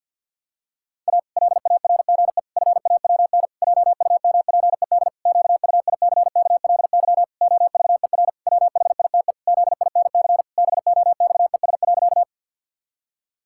Morse code, heard as a keyed tone, about 52 wpm, 700 Hz. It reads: A LARGE FAWN JUMPED QUICKLY OVER WHITE ZINC BOXES.
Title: Pangram Challenge v2 - 50wpm